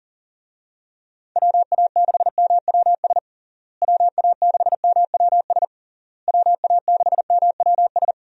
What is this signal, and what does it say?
Morse code, heard as a keyed tone, about 40 wpm, 700 Hz. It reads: WA6MWS WA6MWS WA6MWS
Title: USA Call Signs - 6 Characters - Repeated 3x - 40wpm